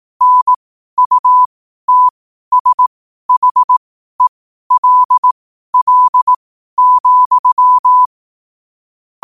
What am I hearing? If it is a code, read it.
Morse code, heard as a keyed tone, about 18 wpm, spaced as for 12 wpm, 1000 Hz. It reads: NUTSHELL,
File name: qotdfiles / 12wpm